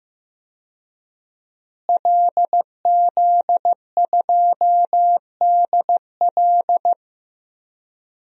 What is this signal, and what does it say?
Morse code, heard as a keyed tone, about 15 wpm, 700 Hz. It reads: LZ2DL
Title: DXCC-100 Call Signs - 5 Characters - Repeated 3x - 15wpm